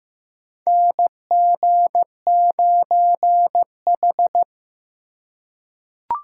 Morse code, heard as a keyed tone, about 15 wpm, 700 Hz. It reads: NG9H
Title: USA Call Signs - 4 Characters - 2x Character Spacing - 15wpm